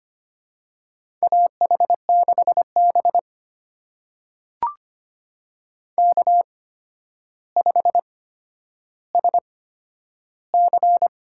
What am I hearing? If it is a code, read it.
Morse code, heard as a keyed tone, about 25 wpm, spaced as for 11 wpm, 700 Hz. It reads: AH6B K5SC
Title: USA Call Signs - 4 Characters - 8x Character Spacing - 25wpm